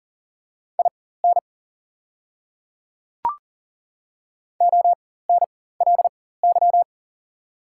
Morse code, heard as a keyed tone, about 40 wpm, spaced as for 19 wpm, 700 Hz. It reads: IN ONLY
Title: 4x Character Spacing - Top 200 Words - 40wpm